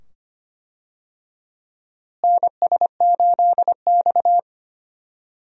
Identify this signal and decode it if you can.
Morse code, heard as a keyed tone, about 25 wpm, 700 Hz. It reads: NS8X
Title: USA Call Signs - 4 Characters - Repeated 2x - 25wpm